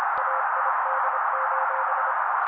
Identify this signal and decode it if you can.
Morse code, heard as a keyed tone, about 26 wpm, 530 Hz, with noise. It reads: AID:
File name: daily / 25wpm noise